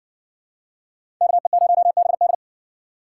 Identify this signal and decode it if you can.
Morse code, heard as a keyed tone, about 60 wpm, 700 Hz. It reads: KE0BD